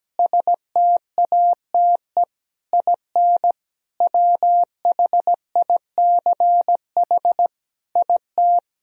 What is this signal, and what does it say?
Morse code, heard as a keyed tone, about 17 wpm, 700 Hz. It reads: STATE IN WHICH IT
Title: Sentences from Top 1000 Words - Review Part 2 - 17wpm